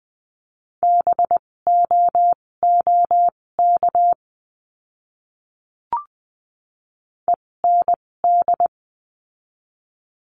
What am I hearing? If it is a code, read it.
Morse code, heard as a keyed tone, about 20 wpm, spaced as for 16 wpm, 700 Hz. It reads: BOOK END
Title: 256 Lesson 42a - Words 12-20wpm